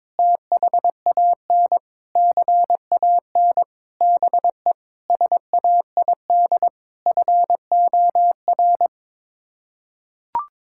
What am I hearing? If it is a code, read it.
Morse code, heard as a keyed tone, about 22 wpm, 700 Hz. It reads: THAN CAN BE SAID FOR
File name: practice / 22wpm / Sets of 8 Words - 22wpm